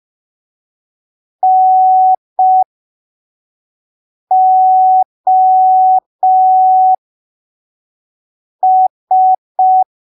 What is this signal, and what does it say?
Morse code, heard as a keyed tone, about 5 wpm, spaced as for 3 wpm, 750 Hz. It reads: NOS
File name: arquivos / texto / texto 05 5ppm